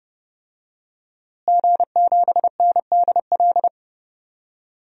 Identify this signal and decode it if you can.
Morse code, heard as a keyed tone, about 30 wpm, 700 Hz. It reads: G7NDL
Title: DXCC-100 Call Signs - 5 Characters - Repeated 3x - 30wpm